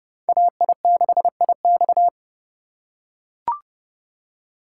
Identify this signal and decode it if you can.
Morse code, heard as a keyed tone, about 31 wpm, 700 Hz. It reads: AI6IX